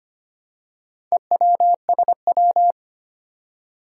Morse code, heard as a keyed tone, about 25 wpm, 700 Hz. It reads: EWSW
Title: Four Challenging Letter-Number Combinations 25wpm